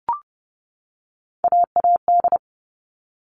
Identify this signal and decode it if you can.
Morse code, heard as a keyed tone, about 29 wpm, 700 Hz. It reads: AAD
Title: All Cut Numbers - 3-Digit 30wpm